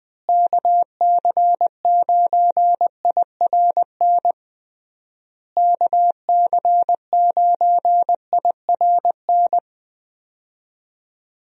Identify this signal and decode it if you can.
Morse code, heard as a keyed tone, about 20 wpm, 700 Hz. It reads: KC9IRN KC9IRN
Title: USA Call Signs - 6 Characters - Repeated 3x - 20wpm